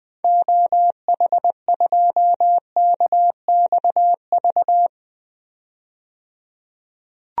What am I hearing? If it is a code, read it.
Morse code, heard as a keyed tone, about 20 wpm, 700 Hz. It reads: OH2KXV
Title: DXCC-100 Call Signs - 6 Characters - 20wpm